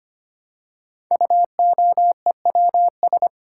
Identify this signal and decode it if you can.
Morse code, heard as a keyed tone, about 25 wpm, 700 Hz. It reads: UOEWS
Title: Code Groups - 5 Letters - 25wpm